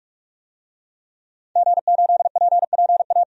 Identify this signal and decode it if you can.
Morse code, heard as a keyed tone, about 46 wpm, 700 Hz. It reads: G8PPA
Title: DXCC-100 Call Signs - 5 Characters - 45wpm